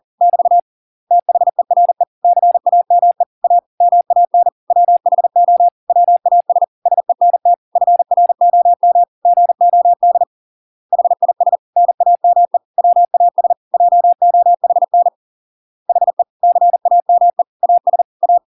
Morse code, heard as a keyed tone, about 40 wpm, 700 Hz. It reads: <BT> THERE CAME A MAN WHO WAS SENT FROM GOD HIS NAME WAS JOHN HE CAME AS A